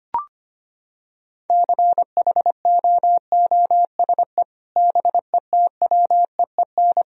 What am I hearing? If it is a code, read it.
Morse code, heard as a keyed tone, about 25 wpm, 700 Hz. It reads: CHOOSE BETWEEN
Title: Sentences from Top 1000 Words - Review Part 4 - 25wpm